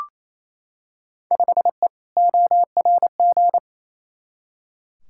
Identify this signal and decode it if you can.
Morse code, heard as a keyed tone, about 28 wpm, 700 Hz. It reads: 5E ORG
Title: Field Day Exchanges Encore - 28wpm